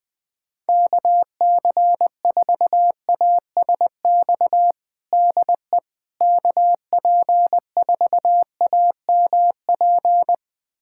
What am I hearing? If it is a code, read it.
Morse code, heard as a keyed tone, about 20 wpm, 700 Hz. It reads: KC4ASX DE KP4AMP
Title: USA Callsign DE Callsign 20wpm